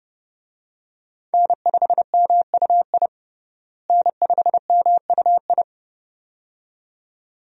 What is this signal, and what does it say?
Morse code, heard as a keyed tone, about 32 wpm, 700 Hz. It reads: N5MUI N5MUI
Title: USA Call Signs - 5 Characters - Repeated 2x - 30wpm